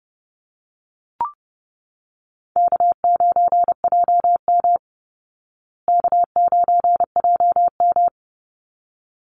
Morse code, heard as a keyed tone, about 29 wpm, 700 Hz. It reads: K9JM K9JM
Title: Easy Call Signs Repeated 2x - 30wpm